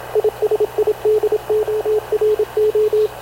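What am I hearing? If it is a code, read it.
Morse code, heard as a keyed tone, about 27 wpm, 430 Hz, with radio noise. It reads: ISIDORO
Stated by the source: Havana Radio CLA press (PX) broadcast, recorded by Radio Officer Paul Dumesnil aboard mv CNAV QUEST/CZDO.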